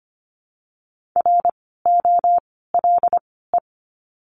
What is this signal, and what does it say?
Morse code, heard as a keyed tone, about 24 wpm, spaced as for 16 wpm, 700 Hz. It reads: ROLE
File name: course / 25:10wpm / 094 Lesson 16f - Letter c - Words 25wpm